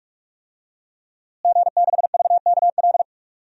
Morse code, heard as a keyed tone, about 42 wpm, 700 Hz. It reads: G6VKL
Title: DXCC-100 Call Signs - 5 Characters - Repeated 3x - 45wpm